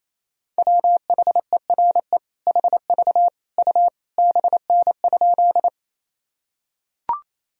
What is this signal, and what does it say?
Morse code, heard as a keyed tone, about 28 wpm, 700 Hz. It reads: WHERE HV U BN?
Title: Sentences from Top 200 Words and Abbreviations 28wpm